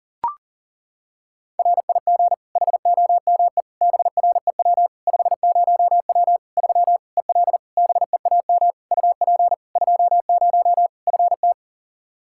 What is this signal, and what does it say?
Morse code, heard as a keyed tone, about 40 wpm, 700 Hz. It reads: RIG HOME BREW 50W 3 EL BEAM UP 20 FT
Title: Practice QSO - Exchange 2 - Info - 40wpm